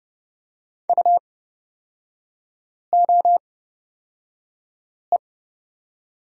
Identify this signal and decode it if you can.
Morse code, heard as a keyed tone, about 30 wpm, spaced as for 5 wpm, 700 Hz. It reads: UOE
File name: Lesson 36e - BT - Three Characters 30wpm